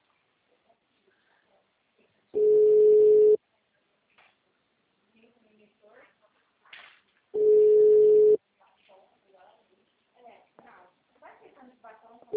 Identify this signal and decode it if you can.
Ringback tone (Europe, 425 Hz), with speech.